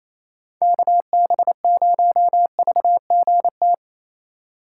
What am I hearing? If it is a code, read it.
Morse code, heard as a keyed tone, about 28 wpm, 700 Hz. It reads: KB0VGT
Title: USA Call Signs - 6 Characters - 4x Character Spacing - 28wpm